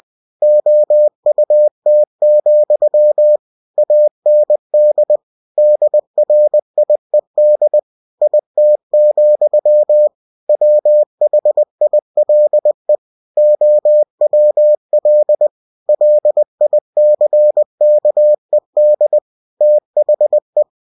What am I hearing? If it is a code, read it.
Morse code, heard as a keyed tone, about 20 wpm, 600 Hz. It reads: OUT, AND DRIED IT, WHILE OWL LICKED THE
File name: cw-books / winnie-the-pooh / winnie-20wpm-page-0051